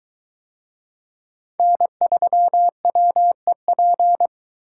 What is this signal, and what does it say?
Morse code, heard as a keyed tone, about 22 wpm, 700 Hz. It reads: N3WEP